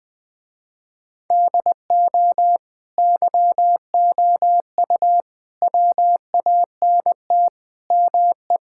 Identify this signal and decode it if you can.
Morse code, heard as a keyed tone, about 20 wpm, 700 Hz. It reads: DO YOU WANT ME
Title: Sentences with 4-Word Phrases - Spoken Beginning - 20wpm